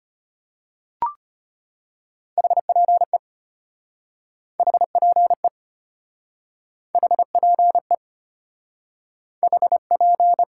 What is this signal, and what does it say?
Morse code, heard as a keyed tone, about 33 wpm, 700 Hz. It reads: HPE HPE HPE HP